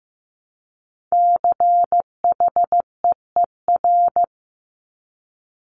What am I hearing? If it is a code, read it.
Morse code, heard as a keyed tone, about 15 wpm, 700 Hz. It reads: CHEER
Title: Multi-letter Phonemes Repeated 3x 15wpm